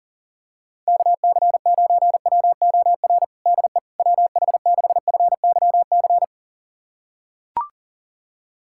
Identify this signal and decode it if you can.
Morse code, heard as a keyed tone, about 40 wpm, 700 Hz. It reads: KC9WOR DE WH6FYC
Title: USA Callsign DE Callsign 40wpm